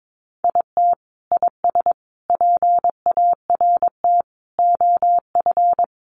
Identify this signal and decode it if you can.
Morse code, heard as a keyed tone, about 22 wpm, 700 Hz. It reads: IT IS PART OF
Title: Sentences from Top 1100 Words 22wpm